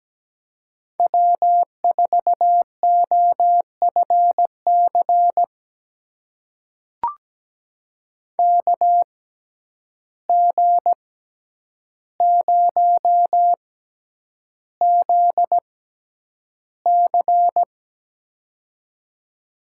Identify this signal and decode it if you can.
Morse code, heard as a keyed tone, about 17 wpm, spaced as for 9 wpm, 700 Hz. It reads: W4OFC KG0ZC
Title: USA Call Signs - 5 Characters - 6x Character Spacing - 17wpm